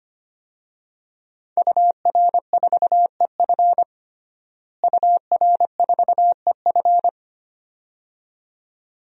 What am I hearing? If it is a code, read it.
Morse code, heard as a keyed tone, about 25 wpm, 700 Hz. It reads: UR4EF UR4EF
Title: DXCC-100 Call Signs - 5 Characters - Repeated 2x - 25wpm